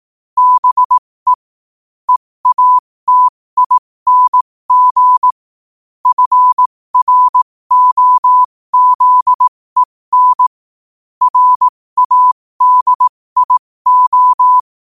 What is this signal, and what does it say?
Morse code, heard as a keyed tone, about 18 wpm, spaced as for 15 wpm, 1000 Hz. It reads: BE EATING FROZEN RADIO
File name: qotdfiles / 15wpm